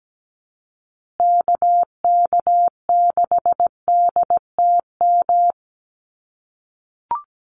Morse code, heard as a keyed tone, about 17 wpm, 700 Hz. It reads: KK6DTM
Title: USA Call Signs Repeated 3x 17wpm